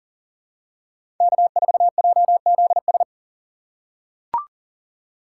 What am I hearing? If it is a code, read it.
Morse code, heard as a keyed tone, about 40 wpm, 700 Hz. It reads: K4JZS
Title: USA Call Signs - 5 Characters - 2x Character Spacing - 40wpm